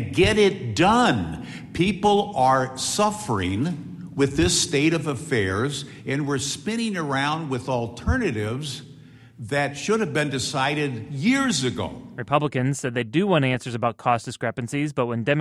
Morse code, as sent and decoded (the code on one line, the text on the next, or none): none